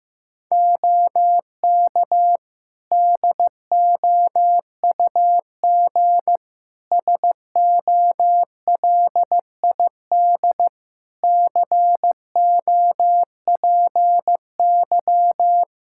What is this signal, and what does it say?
Morse code, heard as a keyed tone, about 16 wpm, 700 Hz. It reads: OK DOUG SOLID COPY
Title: Practice QSO - Exchange 3 - More Info - 15wpm